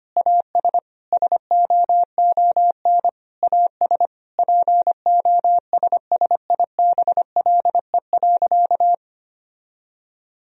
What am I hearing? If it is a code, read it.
Morse code, heard as a keyed tone, about 25 wpm, 700 Hz. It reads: AS SOON AS POSSIBLE.
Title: Sentences from Top 1000 Words - Review Part 4 - 25wpm